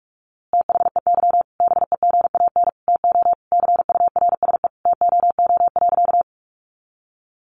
Morse code, heard as a keyed tone, about 46 wpm, 700 Hz. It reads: THEY BEGAN TO CURSE TOO.
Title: A Sound Of Thunder 45wpm